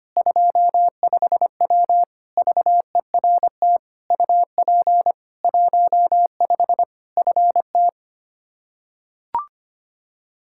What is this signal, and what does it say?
Morse code, heard as a keyed tone, about 25 wpm, 700 Hz. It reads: 25W VERT UP 15 FT
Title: K1USN SST QSO Practice 25wpm